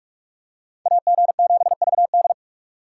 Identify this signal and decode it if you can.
Morse code, heard as a keyed tone, about 42 wpm, 700 Hz. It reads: AG7VD